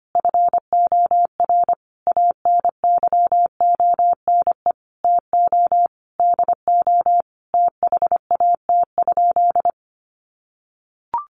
Morse code, heard as a keyed tone, about 25 wpm, 700 Hz. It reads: FOR ANYONE TO DO THAT?
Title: Sentences from Top 900 Words 25wpm